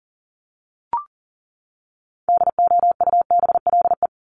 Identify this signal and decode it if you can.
Morse code, heard as a keyed tone, about 40 wpm, 700 Hz. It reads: DOUBLE
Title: Top 1251-1500 Words 40wpm